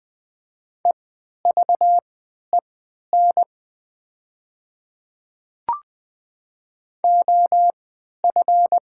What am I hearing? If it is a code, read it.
Morse code, heard as a keyed tone, about 20 wpm, spaced as for 11 wpm, 700 Hz. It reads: EVEN OF
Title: 3x Character Spacing - Top 300 Words - 20wpm